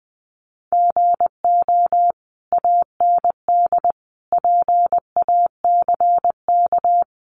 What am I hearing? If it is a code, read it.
Morse code, heard as a keyed tone, about 20 wpm, 700 Hz. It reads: GO AND PACK
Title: Sentences from Top 1300 Words 20wpm